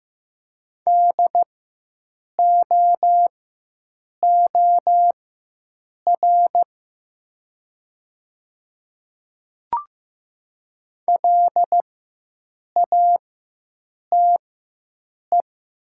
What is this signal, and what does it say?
Morse code, heard as a keyed tone, about 15 wpm, spaced as for 7 wpm, 700 Hz. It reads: DOOR LATE
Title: 4x Character Spacing - Top 300 Words - 15wpm